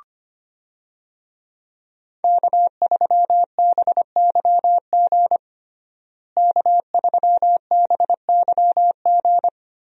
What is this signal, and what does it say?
Morse code, heard as a keyed tone, about 25 wpm, 700 Hz. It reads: K3BYG K3BYG
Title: USA Call Signs - 5 Characters - Repeated 3x - 25wpm